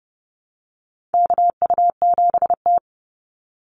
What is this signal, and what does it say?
Morse code, heard as a keyed tone, about 30 wpm, 700 Hz. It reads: KU7T
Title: Call Signs Repeated 3x - 30wpm